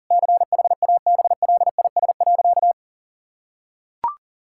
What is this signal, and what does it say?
Morse code, heard as a keyed tone, about 40 wpm, 700 Hz. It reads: CHABLIS.
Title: Pangram Challenge v2 - 40wpm